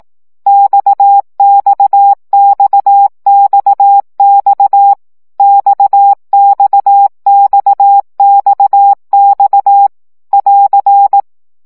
Morse code, heard as a keyed tone, about 18 wpm, 800 Hz. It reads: XXXXX XXXXX <AR>
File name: CW gyakorlas x 15 baud